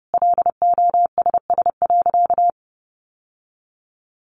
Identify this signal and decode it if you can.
Morse code, heard as a keyed tone, about 30 wpm, 700 Hz. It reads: LOSS.
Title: Sentences from Top 1400 Words 30wpm